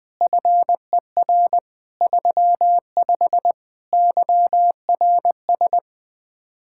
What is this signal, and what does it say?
Morse code, heard as a keyed tone, about 20 wpm, 700 Hz. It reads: FER 35 YRS
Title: Speed-Racing Common QSO Phrases - 17wpm